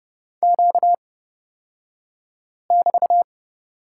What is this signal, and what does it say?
Morse code, heard as a keyed tone, about 31 wpm, spaced as for 5 wpm, 700 Hz. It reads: Q<BT>
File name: Lesson 38d - Letter q - Two Characters 30wpm